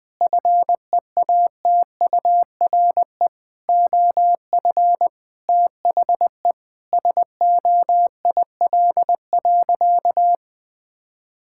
Morse code, heard as a keyed tone, about 19 wpm, 700 Hz. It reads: FEATURE OF THE SOIL.
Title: Sentences from Top 1000 Words - Review Part 5 - 20wpm